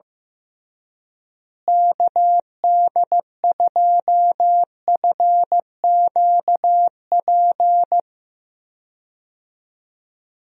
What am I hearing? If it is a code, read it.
Morse code, heard as a keyed tone, about 15 wpm, 700 Hz. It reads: KD2FQP
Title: USA Call Signs - 6 Characters - Repeated 3x - 15wpm